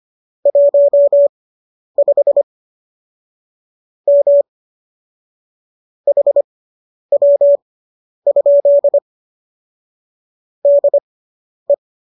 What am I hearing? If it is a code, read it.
Morse code, heard as a keyed tone, about 25 wpm, spaced as for 10 wpm, 570 Hz. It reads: 15 M HW? DE